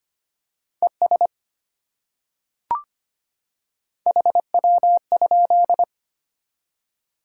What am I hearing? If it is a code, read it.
Morse code, heard as a keyed tone, about 25 wpm, 700 Hz. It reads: ES HW?
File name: Top 20 QSO Elements - 25wpm